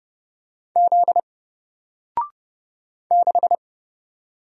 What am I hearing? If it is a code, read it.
Morse code, heard as a keyed tone, about 30 wpm, 700 Hz. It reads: Z 6
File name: Single-Letter-Number-30wpm